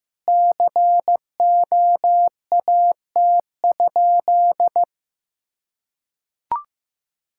Morse code, heard as a keyed tone, about 16 wpm, 700 Hz. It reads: COAT?
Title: Sentences from Top 1600 Words 15wpm